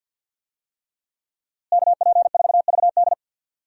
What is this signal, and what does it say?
Morse code, heard as a keyed tone, about 51 wpm, 700 Hz. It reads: KP4VD
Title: DXCC-100 Call Signs - 5 Characters - Repeated 3x - 50wpm